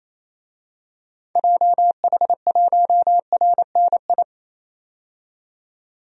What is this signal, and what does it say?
Morse code, heard as a keyed tone, about 28 wpm, 700 Hz. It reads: JH1RNI
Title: DXCC-100 Call Signs - 6 Characters - 28wpm